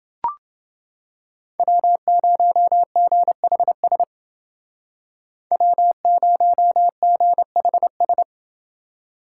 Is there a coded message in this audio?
Yes: Morse code from 1.6 to 8.2 s, heard as a keyed tone, about 29 wpm, 700 Hz. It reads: W0GHS W0GHS